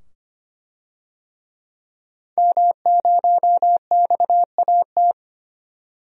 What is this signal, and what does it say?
Morse code, heard as a keyed tone, about 25 wpm, 700 Hz. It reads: M0XAT
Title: DXCC-100 Call Signs - 5 Characters - 25wpm